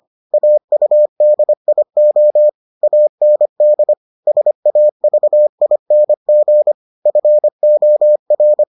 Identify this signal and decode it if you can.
Morse code, heard as a keyed tone, about 25 wpm, 600 Hz. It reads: AUDIO AND SAVING FOR